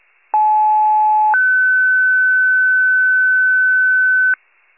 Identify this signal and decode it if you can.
Two-tone page, heard as a one-second tone, then a three-second tone, about 860 then 1500 Hz. It is heard from 0.3 to 4.3 s.
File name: pager / 94 RESCUE OFC / wcalert